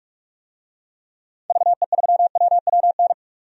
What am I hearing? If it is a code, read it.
Morse code, heard as a keyed tone, about 46 wpm, 700 Hz. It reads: VE3WWN
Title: DXCC-100 Call Signs - 6 Characters - 45wpm